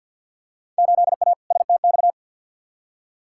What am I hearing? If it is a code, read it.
Morse code, heard as a keyed tone, about 46 wpm, 700 Hz. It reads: 8A STX
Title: Field Day Exchanges Encore - 50wpm